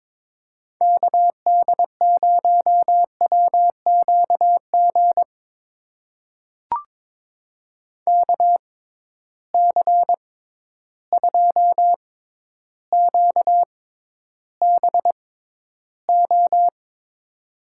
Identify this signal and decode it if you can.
Morse code, heard as a keyed tone, about 22 wpm, spaced as for 11 wpm, 700 Hz. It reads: KD0WQG KC2QBO
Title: USA Call Signs - 6 Characters - 6x Character Spacing - 22wpm